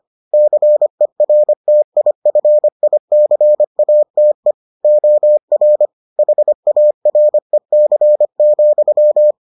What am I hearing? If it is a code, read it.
Morse code, heard as a keyed tone, about 25 wpm, 600 Hz. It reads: CERTIFICATE OR HAREC,